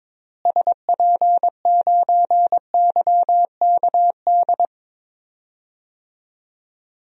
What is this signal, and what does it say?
Morse code, heard as a keyed tone, about 22 wpm, 700 Hz. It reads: SP9YKD
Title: DXCC-100 Call Signs - 6 Characters - Repeated 2x - 22wpm